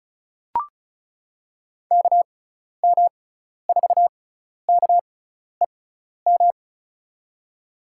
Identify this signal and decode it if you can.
Morse code, heard as a keyed tone, about 36 wpm, spaced as for 12 wpm, 700 Hz. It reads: KM4KEM
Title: USA Call Signs - 6 Characters - 6x Character Spacing - 35wpm